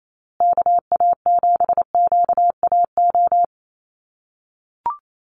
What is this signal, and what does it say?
Morse code, heard as a keyed tone, about 28 wpm, 700 Hz. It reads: KA7QAO
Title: USA Call Signs Repeated 2x 28wpm